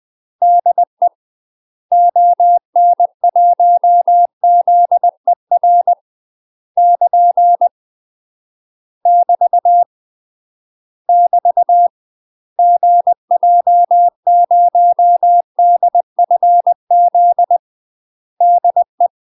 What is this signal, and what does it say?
Morse code, heard as a keyed tone, about 20 wpm, 700 Hz. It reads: DE ON1ZER <KN> <BT> <BT> GJ0DFZ DE
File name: UK MASTER QSOs TRAINING / 20WPM / UK MASTER QSOs 0066